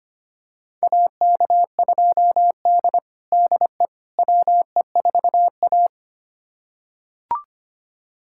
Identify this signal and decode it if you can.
Morse code, heard as a keyed tone, about 25 wpm, 700 Hz. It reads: AK2D DE WE4A
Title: USA Callsign DE Callsign - 4 Characters - 25wpm